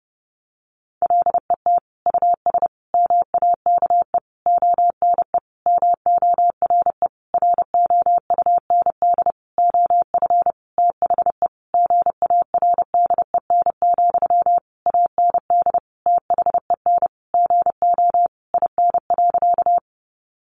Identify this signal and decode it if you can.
Morse code, heard as a keyed tone, about 30 wpm, 700 Hz. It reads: LET US MAKE ONE MORE ROUND OF THE GARDEN, AND THEN GO IN.
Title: Sentences from Top 700 Words 30wpm